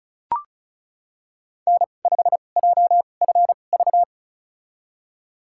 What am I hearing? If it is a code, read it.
Morse code, heard as a keyed tone, about 36 wpm, spaced as for 25 wpm, 700 Hz. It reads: N5JFV
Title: USA Call Signs - 5 Characters - 2x Character Spacing - 35wpm